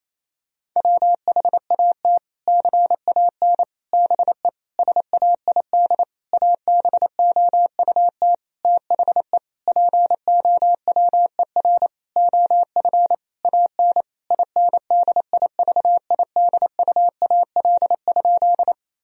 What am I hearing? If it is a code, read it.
Morse code, heard as a keyed tone, about 28 wpm, 700 Hz. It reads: WHAT CAN BE SAID ABOUT THE POWER OF AN INDIVIDUAL?
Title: Sentences from Top 1000 Words - Review Part 4 - 28wpm